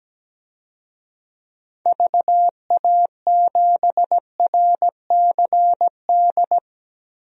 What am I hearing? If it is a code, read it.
Morse code, heard as a keyed tone, about 17 wpm, 700 Hz. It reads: VA7RCD
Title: DXCC-100 Call Signs - 6 Characters - 17wpm